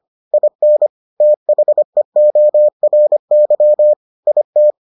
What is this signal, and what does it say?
Morse code, heard as a keyed tone, about 25 wpm, 600 Hz. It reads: IN THEORY IT